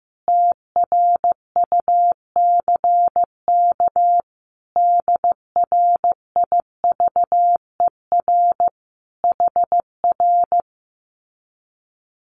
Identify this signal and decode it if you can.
Morse code, heard as a keyed tone, about 15 wpm, 700 Hz. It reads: TRUCK DRIVER HR
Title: Common QSO Phrases Encore 15wpm